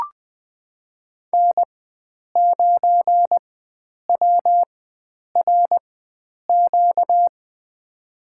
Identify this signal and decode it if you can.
Morse code, heard as a keyed tone, about 21 wpm, spaced as for 9 wpm, 700 Hz. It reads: N9WRQ
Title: USA Call Signs - 5 Characters - 4x Character Spacing - 20wpm